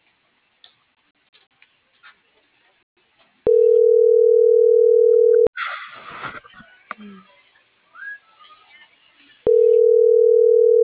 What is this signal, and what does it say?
Ringback tone (North America), 440 + 480 Hz.